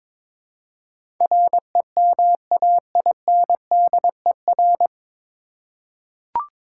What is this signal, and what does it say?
Morse code, heard as a keyed tone, about 23 wpm, 700 Hz. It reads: REMAINDER
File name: 9 Letter Words - First 3 Letters Given - 22wpm